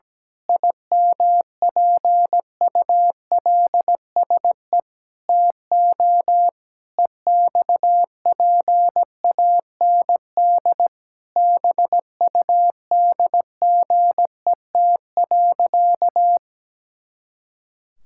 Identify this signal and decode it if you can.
Morse code, heard as a keyed tone, about 17 wpm, 700 Hz. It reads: IMPULSE TO EXPAND BUDGET.
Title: Pangram Challenge v2 - 17wpm